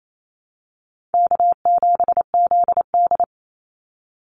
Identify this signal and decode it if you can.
Morse code, heard as a keyed tone, about 28 wpm, 700 Hz. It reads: K7ZD
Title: Easy Call Signs Repeated 3x - 28wpm